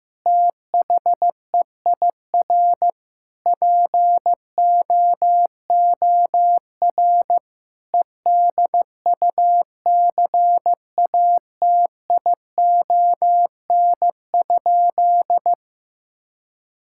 Morse code, heard as a keyed tone, about 15 wpm, 700 Hz. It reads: THEIR POOR EDUCATION?
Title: Sentences from Top 1000 Words - Review Part 6 - 15wpm